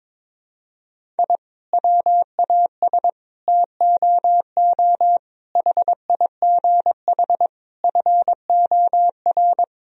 Morse code, heard as a keyed tone, about 22 wpm, 700 Hz. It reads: I WAS TOO HIGH FOR
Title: Sentences from Top 1000 Words - Review Part 1 - 22wpm